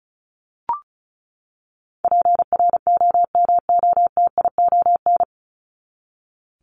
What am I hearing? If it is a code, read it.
Morse code, heard as a keyed tone, about 34 wpm, 700 Hz. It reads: PROMOTION
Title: Top 1501-1750 Words 35wpm